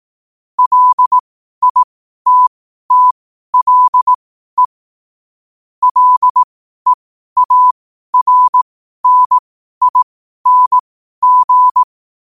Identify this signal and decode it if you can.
Morse code, heard as a keyed tone, about 18 wpm, spaced as for 12 wpm, 1000 Hz. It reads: LITTLE LEARNING
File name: qotdfiles / 12wpm